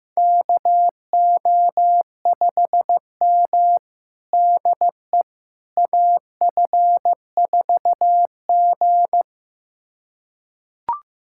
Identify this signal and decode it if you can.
Morse code, heard as a keyed tone, about 15 wpm, 700 Hz. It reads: KO5M DE AF4G
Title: USA Callsign DE Callsign - 4 Characters - 15wpm